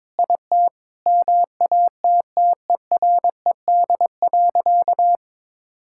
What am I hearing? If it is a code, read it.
Morse code, heard as a keyed tone, about 22 wpm, 700 Hz. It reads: IT MATTERED.
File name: practice / 22wpm / Sentences with 4-Word Phrases - 22wpm